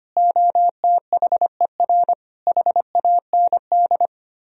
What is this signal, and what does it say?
Morse code, heard as a keyed tone, about 32 wpm, 700 Hz. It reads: OTHER HAND